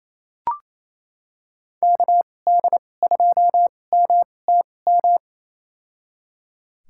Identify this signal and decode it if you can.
Morse code, heard as a keyed tone, about 28 wpm, spaced as for 20 wpm, 700 Hz. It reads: KD2MTM
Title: USA Call Signs - 2x Character Spacing - 28wpm